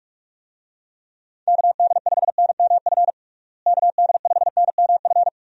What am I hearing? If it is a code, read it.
Morse code, heard as a keyed tone, about 41 wpm, 700 Hz. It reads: KD5NMF KD5NMF
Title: USA Call Signs - 6 Characters - Repeated 3x - 45wpm